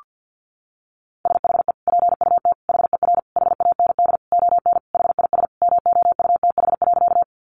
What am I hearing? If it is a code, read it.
Morse code, heard as a keyed tone, about 50 wpm, 700 Hz. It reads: SHE PUT HER HAND ON HIS MOUTH.
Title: Sentences from Top 1500 Words 50wpm